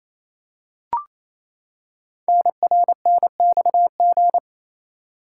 Morse code, heard as a keyed tone, about 28 wpm, 700 Hz. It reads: NRNXG
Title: Code Groups - 5 Letters - 28wpm